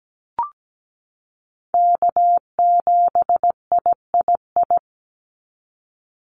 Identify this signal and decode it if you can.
Morse code, heard as a keyed tone, about 17 wpm, 700 Hz. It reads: K7III
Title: NA SOTA Chaser Callsigns 17wpm